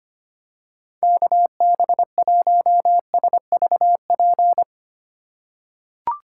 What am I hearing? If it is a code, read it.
Morse code, heard as a keyed tone, about 25 wpm, 700 Hz. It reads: KB1SVP
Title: USA Call Signs - 6 Characters - 2x Character Spacing - 25wpm